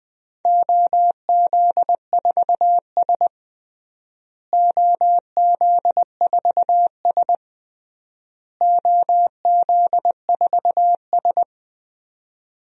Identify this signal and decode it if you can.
Morse code, heard as a keyed tone, about 20 wpm, 700 Hz. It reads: OZ4S OZ4S OZ4S
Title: DXCC-100 Call Signs - 4 Characters - Repeated 3x - 20wpm